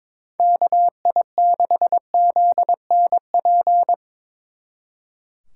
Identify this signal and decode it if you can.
Morse code, heard as a keyed tone, about 24 wpm, 700 Hz. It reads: KI6ZNP
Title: Speed Racing - USA 2x3 Call Signs 22wpm